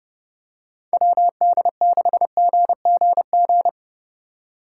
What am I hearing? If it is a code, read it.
Morse code, heard as a keyed tone, about 29 wpm, 700 Hz. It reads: WD6GGG
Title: USA Call Signs - 4x Character Spacing - 30wpm